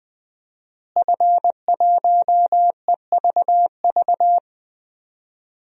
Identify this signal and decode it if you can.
Morse code, heard as a keyed tone, about 20 wpm, 700 Hz. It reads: F1EVV
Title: DXCC-100 Call Signs - 5 Characters - Repeated 2x - 20wpm